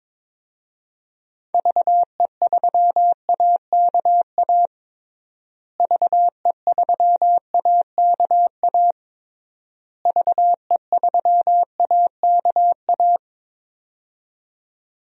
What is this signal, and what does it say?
Morse code, heard as a keyed tone, about 21 wpm, 700 Hz. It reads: VE3AKA VE3AKA VE3AKA
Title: DXCC-100 Call Signs - 6 Characters - Repeated 3x - 22wpm